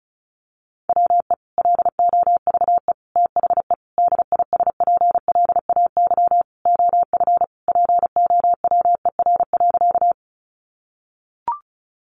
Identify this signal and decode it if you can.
Morse code, heard as a keyed tone, about 34 wpm, 700 Hz. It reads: WE LOVE THE DISPLAY OF POWER.
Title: Sentences from Top 1200 Words 35wpm